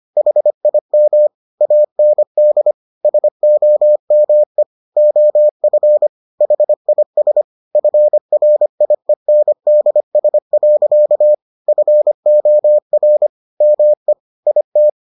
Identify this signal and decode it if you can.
Morse code, heard as a keyed tone, about 25 wpm, 600 Hz. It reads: HIM AND SOME OF HIS FRIENDS. FOR ME IT